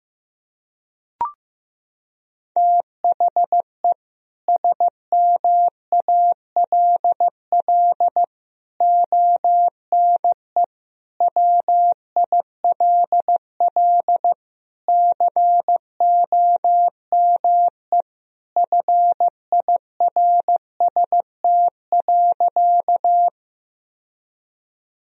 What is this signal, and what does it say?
Morse code, heard as a keyed tone, about 16 wpm, 700 Hz. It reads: THE SMALL ONE WILL COME FIRST.
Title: Sentences from Top 1000 Words - Review Part 1 - 15wpm